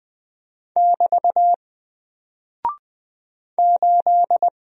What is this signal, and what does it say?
Morse code, heard as a keyed tone, about 20 wpm, 700 Hz. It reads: <BT> 8
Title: Single Character (Letter:Number:Punctuation:Prosign) 20wpm